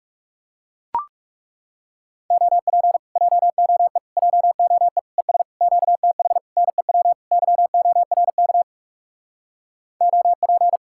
Morse code, heard as a keyed tone, about 40 wpm, 700 Hz. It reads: OP JOE JOE ES QTH NEW YORK OP